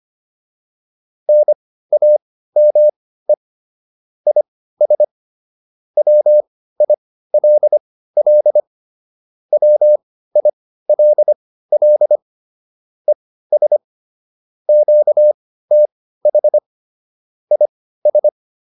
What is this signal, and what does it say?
Morse code, heard as a keyed tone, about 25 wpm, spaced as for 15 wpm, 600 Hz. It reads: NAME IS WILL WILL ES QTH IS
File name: QSO409 15